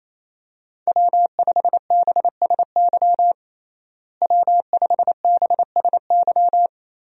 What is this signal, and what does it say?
Morse code, heard as a keyed tone, about 28 wpm, 700 Hz. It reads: W5BSY W5BSY
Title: USA Call Signs - 5 Characters - Repeated 2x - 28wpm